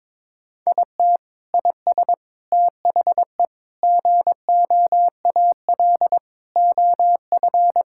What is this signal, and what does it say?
Morse code, heard as a keyed tone, about 22 wpm, 700 Hz. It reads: IT IS THE GOAL OF
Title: Sentences from Top 1000 Words - Review Part 5 - 22wpm